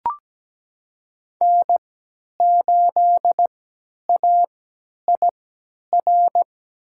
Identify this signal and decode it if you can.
Morse code, heard as a keyed tone, about 17 wpm, spaced as for 10 wpm, 700 Hz. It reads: N8AIR
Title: USA Call Signs - 5 Characters - 3x Character Spacing - 17wpm